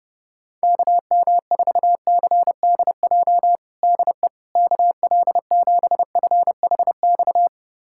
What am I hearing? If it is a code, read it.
Morse code, heard as a keyed tone, about 30 wpm, 700 Hz. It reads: KM4CDJ DE KL7FHX
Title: USA Callsign DE Callsign - 6 Characters - 30wpm